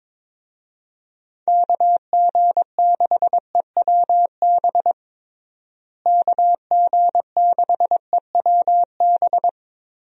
Morse code, heard as a keyed tone, about 22 wpm, 700 Hz. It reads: KG6EWB KG6EWB
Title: USA Call Signs - 6 Characters - Repeated 3x - 22wpm